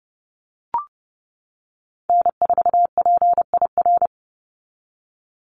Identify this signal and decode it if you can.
Morse code, heard as a keyed tone, about 30 wpm, 700 Hz. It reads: N4PIR
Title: Call Signs - 30wpm